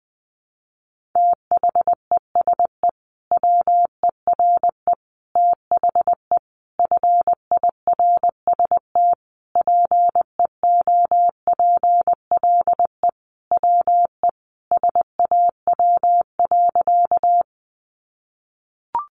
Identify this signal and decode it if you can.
Morse code, heard as a keyed tone, about 20 wpm, 700 Hz. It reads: THESE WERE THE FIRST PEOPLE WE SAW.
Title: Sentences from Top 300 Words 20wpm